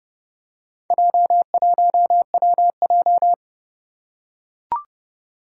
Morse code, heard as a keyed tone, about 28 wpm, 700 Hz. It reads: J1WJ
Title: Four Challenging Letter-Number Combinations 30wpm